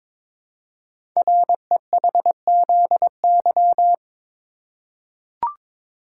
Morse code, heard as a keyed tone, about 23 wpm, 700 Hz. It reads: REHZY
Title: Code Groups - 5 Letters - 22wpm